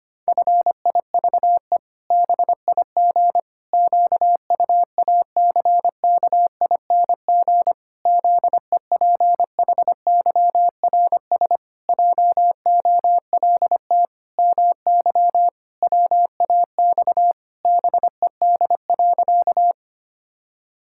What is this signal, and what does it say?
Morse code, heard as a keyed tone, about 25 wpm, 700 Hz. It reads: FIVE BIG QUACKING ZEPHYRS JOLT MY WAX BED.
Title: Pangram Challenge v2 - 25wpm